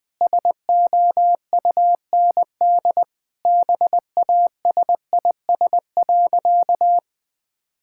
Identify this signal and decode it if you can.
Morse code, heard as a keyed tone, about 21 wpm, 700 Hz. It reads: SOUND BASIS.
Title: Sentences from Top 1000 Words - Review Part 5 - 20wpm